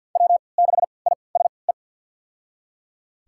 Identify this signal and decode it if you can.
Morse code, heard as a keyed tone, about 51 wpm, spaced as for 28 wpm, 700 Hz. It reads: W6ISE
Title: USA Call Signs - 5 Characters - 3x Character Spacing - 50wpm